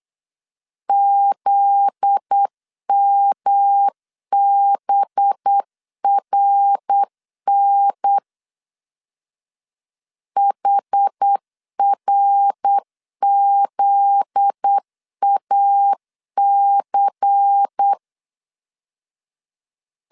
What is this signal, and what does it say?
Morse code, heard as a keyed tone, about 8 wpm, 800 Hz. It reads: ZMBRN HRZAC
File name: LEZIONI / 7 GF9Z / 7 GF9Z 8wpm 3213